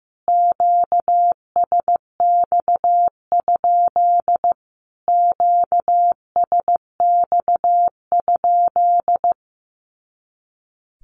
Morse code, heard as a keyed tone, about 15 wpm, 700 Hz. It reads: QSX? QSX?
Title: Q-Codes Repeated 3x 15wpm